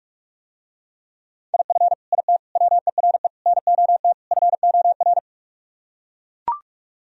Morse code, heard as a keyed tone, about 44 wpm, 700 Hz. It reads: IF IT WERE NOT FOR